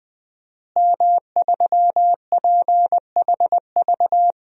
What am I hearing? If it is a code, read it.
Morse code, heard as a keyed tone, about 20 wpm, 700 Hz. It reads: M3PHV
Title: DXCC-100 Call Signs - 5 Characters - Repeated 2x - 20wpm